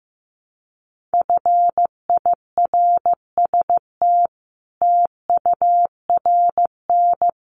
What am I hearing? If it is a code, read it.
Morse code, heard as a keyed tone, about 15 wpm, 700 Hz. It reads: FIRST TURN
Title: Sentences from Top 300 Words 15wpm